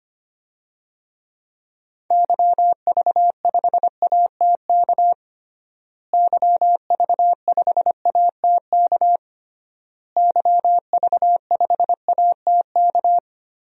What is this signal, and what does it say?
Morse code, heard as a keyed tone, about 25 wpm, 700 Hz. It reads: YV5ATK YV5ATK YV5ATK
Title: DXCC-100 Call Signs - 6 Characters - Repeated 3x - 25wpm